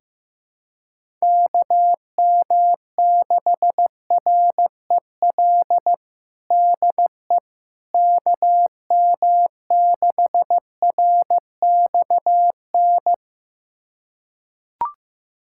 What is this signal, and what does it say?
Morse code, heard as a keyed tone, about 15 wpm, 700 Hz. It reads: KM6REL DE KM6RXN
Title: USA Callsign DE Callsign - 6 Characters - 15wpm